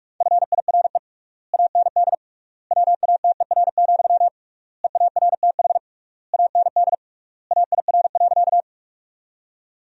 Morse code, heard as a keyed tone, about 46 wpm, 700 Hz. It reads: FIRE AND WATER, EARTH AND AIR.
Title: 3x Word-Spacing - Sentences from Top 400 Words - 45wpm